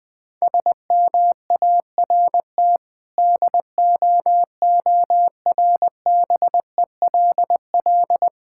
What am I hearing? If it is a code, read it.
Morse code, heard as a keyed tone, about 20 wpm, 690 Hz. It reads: SMART DOORBELL
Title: Difficult Words with Hints - 20wpm